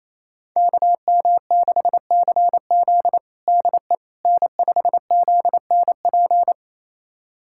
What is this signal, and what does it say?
Morse code, heard as a keyed tone, about 28 wpm, 700 Hz. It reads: KM6CZ DE N5ZNP
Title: USA Callsign DE Callsign - 5 Characters - 28wpm